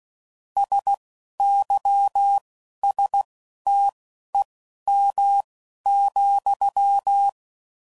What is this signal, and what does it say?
Morse code, heard as a keyed tone, about 16 wpm, spaced as for 12 wpm, 800 Hz. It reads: SYSTEM,
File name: CW Code Groups @10WPM / Technical Subjects / SwPwrSup